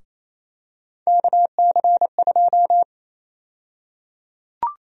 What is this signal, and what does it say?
Morse code, heard as a keyed tone, about 28 wpm, 700 Hz. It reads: KC2
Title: USA Call Sign Prefixes - Repeated 2x - 28wpm